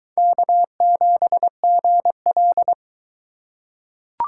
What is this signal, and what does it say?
Morse code, heard as a keyed tone, about 22 wpm, 700 Hz. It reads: K7GL